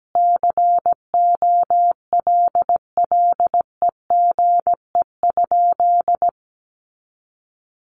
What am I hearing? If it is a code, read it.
Morse code, heard as a keyed tone, about 17 wpm, 700 Hz. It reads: COLLEGE?
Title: Sentences from Top 1300 Words 17wpm